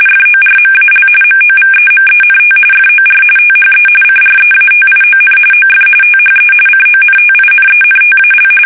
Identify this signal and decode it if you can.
RTTY, about 45.45 baud. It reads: FORMAL MOVEMENT FOR THE CREATION OF THE CONTINENTAL
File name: fsk850 / usnavy1